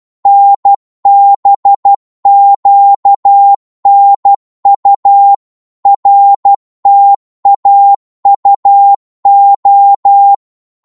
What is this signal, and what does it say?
Morse code, heard as a keyed tone, about 12 wpm, 800 Hz. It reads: NBQNU RTAUO
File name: Morse / ltr-12wpm